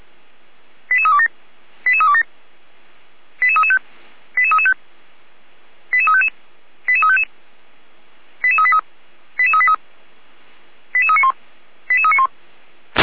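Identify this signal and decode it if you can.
Five-tone selective call (ZVEI), heard as quick runs of five tones, tones of about 70 ms. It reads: 88327 88327 88335 88335 88355 88355 88383 88383 88391 88391
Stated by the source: Five-tone ZVEI selective call received on an SDR.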